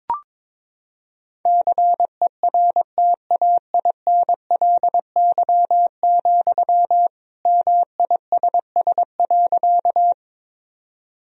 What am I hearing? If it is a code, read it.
Morse code, heard as a keyed tone, about 22 wpm, 700 Hz. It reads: CERTAINLY, MISS.
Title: Sentences from Top 1000 Words - Review Part 3 - 22wpm